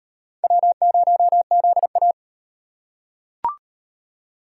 Morse code, heard as a keyed tone, about 38 wpm, 700 Hz. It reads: W0ZA